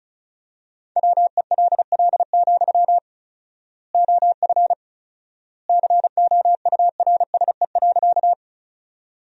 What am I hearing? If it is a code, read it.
Morse code, heard as a keyed tone, about 36 wpm, 700 Hz. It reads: WELL, OF COURSE.
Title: 4x Word-Spacing - Sentences from Top 500 Words - 35wpm